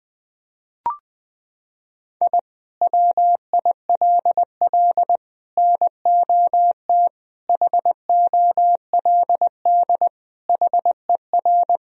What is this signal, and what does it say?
Morse code, heard as a keyed tone, about 20 wpm, 700 Hz. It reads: I WILL NOT HOLD HER
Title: Sentences from Top 1000 Words - Review Part 2 - 20wpm